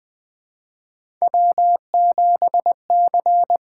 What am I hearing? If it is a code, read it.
Morse code, heard as a keyed tone, about 20 wpm, 700 Hz. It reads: W7C
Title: Three Letter-Number Combinations - 20wpm